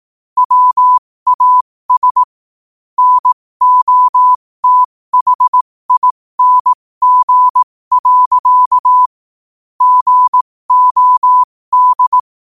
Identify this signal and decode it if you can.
Morse code, heard as a keyed tone, about 18 wpm, spaced as for 15 wpm, 1000 Hz. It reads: WAS NOTHING. GOD
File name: qotdfiles / 15wpm